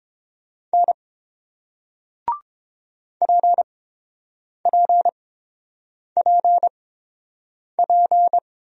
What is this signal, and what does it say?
Morse code, heard as a keyed tone, about 30 wpm, spaced as for 7 wpm, 700 Hz. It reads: N PPPP